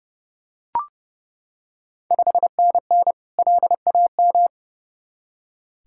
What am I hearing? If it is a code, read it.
Morse code, heard as a keyed tone, about 30 wpm, 700 Hz. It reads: 5NN LAM
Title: Ontario QSO Party Exchange 30wpm